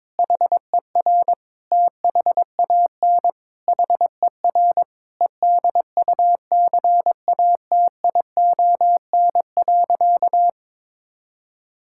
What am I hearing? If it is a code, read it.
Morse code, heard as a keyed tone, about 22 wpm, 700 Hz. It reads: HER THAN HER EDUCATION.
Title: Sentences from Top 1000 Words - Review Part 6 - 22wpm